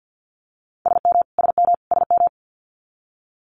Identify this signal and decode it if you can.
Morse code, heard as a keyed tone, about 50 wpm, 700 Hz. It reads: SM SM SM
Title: Multi-letter Phonemes Repeated 3x 50wpm